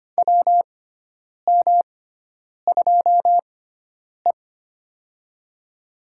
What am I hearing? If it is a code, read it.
Morse code, heard as a keyed tone, about 25 wpm, spaced as for 9 wpm, 700 Hz. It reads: WM2E
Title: USA Call Signs - 6x Character Spacing - 25wpm